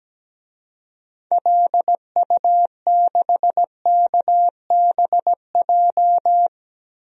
Morse code, heard as a keyed tone, about 17 wpm, 700 Hz. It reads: LU6KBJ
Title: DXCC-100 Call Signs - 6 Characters - Repeated 3x - 17wpm